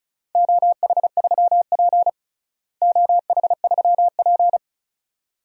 Morse code, heard as a keyed tone, about 34 wpm, 700 Hz. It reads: OH3P OH3P
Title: DXCC-100 Call Signs - 4 Characters - Repeated 2x - 35wpm